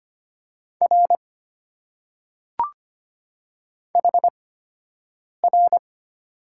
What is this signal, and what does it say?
Morse code, heard as a keyed tone, about 25 wpm, spaced as for 7 wpm, 690 Hz. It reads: R HR
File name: Top 90 QSO Elements - 8x Character Spacing - 25wpm